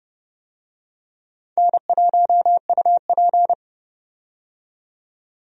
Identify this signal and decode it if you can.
Morse code, heard as a keyed tone, about 30 wpm, 700 Hz. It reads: N1UP
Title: USA Call Signs - 4 Characters - 30wpm